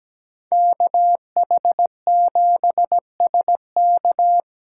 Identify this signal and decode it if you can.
Morse code, heard as a keyed tone, about 20 wpm, 700 Hz. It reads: KH7SK
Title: Speed Racing - USA Call Signs 17wpm